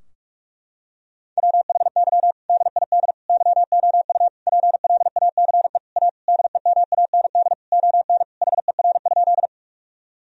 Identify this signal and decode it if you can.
Morse code, heard as a keyed tone, about 46 wpm, 700 Hz. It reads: WHY DID YOU PLACE A DEMAND ON HER?
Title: Sentences from Top 1000 Words - Review Part 4 - 45wpm